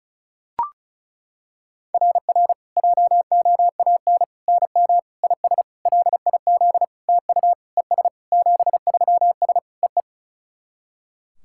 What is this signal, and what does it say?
Morse code, heard as a keyed tone, about 34 wpm, 690 Hz. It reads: RR JOAN NM IS LIZ TU ES 73S EE
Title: SOTA QSO Practice 35wpm